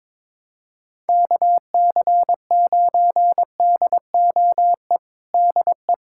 Morse code, heard as a keyed tone, about 22 wpm, 700 Hz. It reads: KC9DOE DE
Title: USA Callsign DE Callsign - 6 Characters - 22wpm